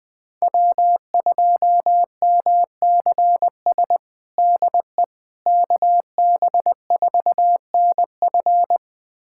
Morse code, heard as a keyed tone, about 20 wpm, 700 Hz. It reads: W2MCS DE KB4NF
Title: USA Callsign DE Callsign - 5 Characters - 20wpm